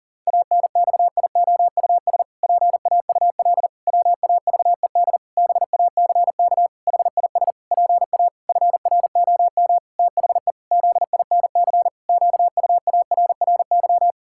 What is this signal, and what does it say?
Morse code, heard as a keyed tone, about 40 wpm, 700 Hz. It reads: ANXIOUS PAUL WAVED BACK HIS PA FROM THE ZINC QUARRY
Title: Pangram Challenge v2 - 40wpm